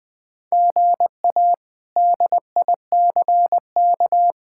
Morse code, heard as a keyed tone, about 21 wpm, 700 Hz. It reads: GA DICK
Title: K1USN SST QSO Practice - 20wpm